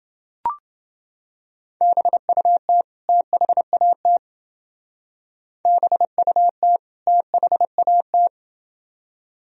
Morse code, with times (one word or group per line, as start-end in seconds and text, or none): 1.8-2.8 BUT
3.1-4.2 THAT
5.7-6.8 BUT
7.1-8.3 THAT